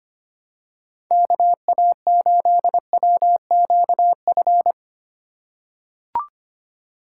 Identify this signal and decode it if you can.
Morse code, heard as a keyed tone, about 25 wpm, 700 Hz. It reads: KA8WQF
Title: USA Call Signs - 6 Characters - 2x Character Spacing - 25wpm